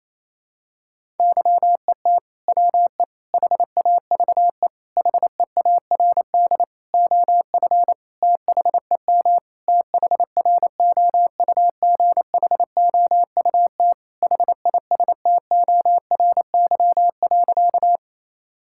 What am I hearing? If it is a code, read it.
Morse code, heard as a keyed tone, about 28 wpm, 700 Hz. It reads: YET WE HAVE HEARD OF THEM THROUGHOUT HISTORY.
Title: Sentences from Top 1000 Words - Review Part 5 - 28wpm